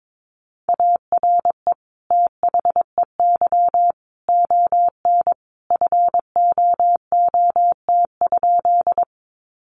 Morse code, heard as a keyed tone, about 22 wpm, 700 Hz. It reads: ARE THEY ON FOOT?
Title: Sentences from Top 500 Words 22wpm